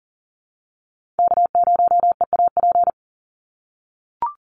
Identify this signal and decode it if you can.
Morse code, heard as a keyed tone, about 40 wpm, 700 Hz. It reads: K0EAP
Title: NA SOTA Chaser Callsigns 40wpm